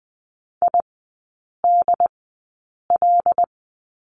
Morse code, heard as a keyed tone, about 20 wpm, spaced as for 8 wpm, 700 Hz. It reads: IDL
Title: 087 Lesson 15e - Number 5 - Three Characters 20wpm